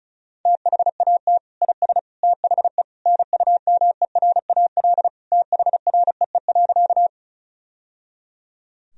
Morse code, heard as a keyed tone, about 34 wpm, 700 Hz. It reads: THAT IS THE NUMERAL THREE.
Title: Sentences from Top 1000 Words - Review Part 2 - 35wpm